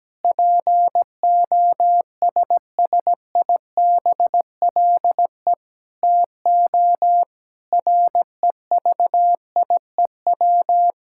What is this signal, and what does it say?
Morse code, heard as a keyed tone, about 17 wpm, 700 Hz. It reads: POSSIBLE TO REVIEW
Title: Sentences from Top 1000 Words - Review Part 6 - 17wpm